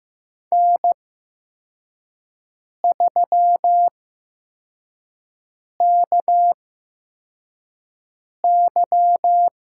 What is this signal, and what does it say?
Morse code, heard as a keyed tone, about 15 wpm, spaced as for 4 wpm, 700 Hz. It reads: N3KY
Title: USA Call Signs - 5 Characters - 8x Character Spacing - 15wpm